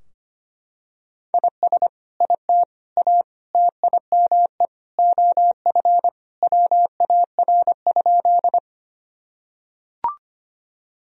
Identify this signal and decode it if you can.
Morse code, heard as a keyed tone, about 25 wpm, 700 Hz. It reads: IS IT A TIME OF WAR?
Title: Sentences from Top 1000 Words - Review Part 2 - 25wpm